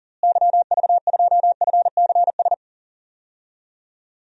Morse code, heard as a keyed tone, about 40 wpm, 700 Hz. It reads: YV2FCS
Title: DXCC-100 Call Signs - 6 Characters - Repeated 2x - 40wpm